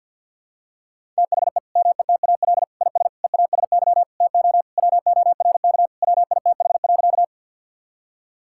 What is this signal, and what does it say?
Morse code, heard as a keyed tone, about 50 wpm, 700 Hz. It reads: THE METAL IS EASY TO WORK WITH.
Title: Sentences from Top 1000 Words - Review Part 6 - 50wpm